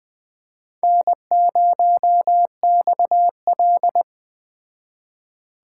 Morse code, heard as a keyed tone, about 20 wpm, 700 Hz. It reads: N0XL
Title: Speed Racing - USA Call Signs 17wpm